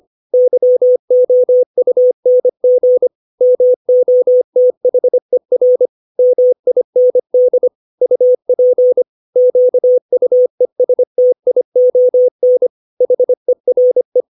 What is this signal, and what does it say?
Morse code, heard as a keyed tone, about 25 wpm, 500 Hz. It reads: YOUNG MOTHER MIND UP QUESTION HERE
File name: 632 words / 632 Words 25WPM